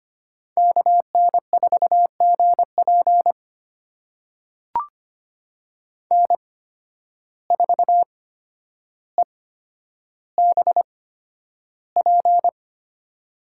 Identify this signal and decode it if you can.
Morse code, heard as a keyed tone, about 25 wpm, spaced as for 11 wpm, 700 Hz. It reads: KN4GP N4EBP
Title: USA Call Signs - 5 Characters - 8x Character Spacing - 25wpm